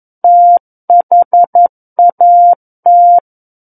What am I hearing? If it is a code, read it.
Morse code, heard as a keyed tone, about 11 wpm, 700 Hz. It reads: THAT
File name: 11 WPM